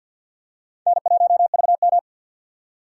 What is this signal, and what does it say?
Morse code, heard as a keyed tone, about 52 wpm, 700 Hz. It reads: N1VM
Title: USA Call Signs - 4 Characters - 6x Character Spacing - 50wpm